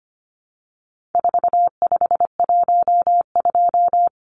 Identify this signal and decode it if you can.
Morse code, heard as a keyed tone, about 25 wpm, 700 Hz. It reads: 4512
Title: Numbers - 4-Digit 25wpm